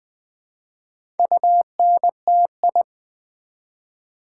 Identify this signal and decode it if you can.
Morse code, heard as a keyed tone, about 20 wpm, 700 Hz. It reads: UNTI
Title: Build A Word - 1 Letter At A Time - 5-Letter Words - 20wpm